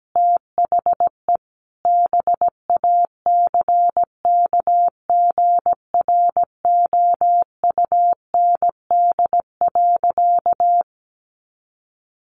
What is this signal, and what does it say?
Morse code, heard as a keyed tone, about 17 wpm, 700 Hz. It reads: THE BACKGROUND.
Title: Sentences from Top 1300 Words 17wpm